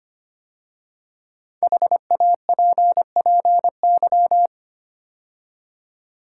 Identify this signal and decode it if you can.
Morse code, heard as a keyed tone, about 25 wpm, 700 Hz. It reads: HAPPY
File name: 5 Letter Words - First 3 Letters Given - 25wpm